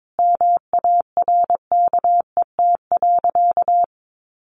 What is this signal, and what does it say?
Morse code, heard as a keyed tone, about 22 wpm, 700 Hz. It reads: MARKET.
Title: Sentences from Top 1400 Words 22wpm